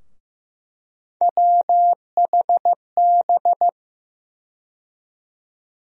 Morse code, heard as a keyed tone, about 15 wpm, 700 Hz. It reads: WHB
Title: USA Call Sign Postfixes - 3 Letters 15wpm